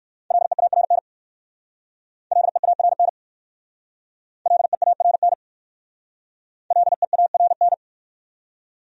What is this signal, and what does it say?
Morse code, heard as a keyed tone, about 59 wpm, 700 Hz. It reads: LEARN LEARN LEARN LEARN